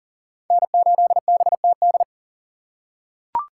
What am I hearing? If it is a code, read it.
Morse code, heard as a keyed tone, about 40 wpm, 700 Hz. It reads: N8BTD
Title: USA Call Signs - 5 Characters - 4x Character Spacing - 40wpm